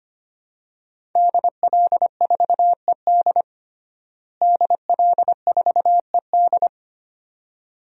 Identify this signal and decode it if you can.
Morse code, heard as a keyed tone, about 25 wpm, 700 Hz. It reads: DL4ED DL4ED
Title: DXCC-100 Call Signs - 5 Characters - Repeated 2x - 25wpm